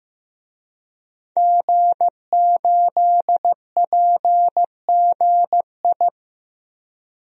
Morse code, heard as a keyed tone, about 15 wpm, 700 Hz. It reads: G8PGI
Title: DXCC-100 Call Signs - 5 Characters - 15wpm